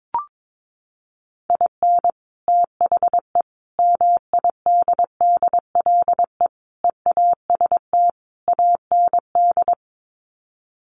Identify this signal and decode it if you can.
Morse code, heard as a keyed tone, about 22 wpm, 700 Hz. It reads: IN THE MIDDLE EAST AND
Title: Sets of 5 Words Encore 22wpm